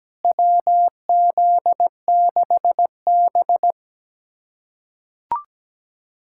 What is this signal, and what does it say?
Morse code, heard as a keyed tone, about 17 wpm, 700 Hz. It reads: WZ6B
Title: USA Callsign DE Callsign - 4 Characters - 17wpm